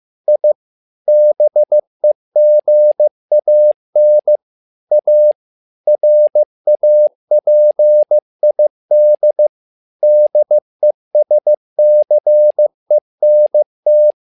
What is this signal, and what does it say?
Morse code, heard as a keyed tone, about 15 wpm, 600 Hz. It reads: I BEGAN A RAPID DESCENT
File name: learning center / tower trouble / tower trouble 15wpm